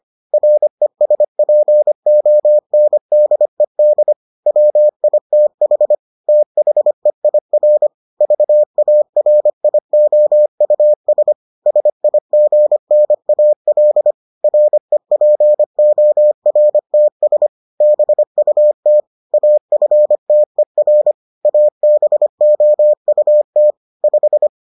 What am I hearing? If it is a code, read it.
Morse code, heard as a keyed tone, about 25 wpm, 600 Hz. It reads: RESPONDED WITH THEIR VARIOUS SIGNAL REPORTS BUT AFTER ABOUT 5